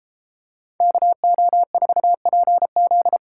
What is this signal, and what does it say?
Morse code, heard as a keyed tone, about 30 wpm, 700 Hz. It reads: KO4PZ